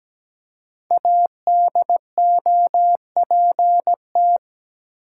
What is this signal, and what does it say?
Morse code, heard as a keyed tone, about 17 wpm, 700 Hz. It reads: ADOPT
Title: Five Letter Words - 17wpm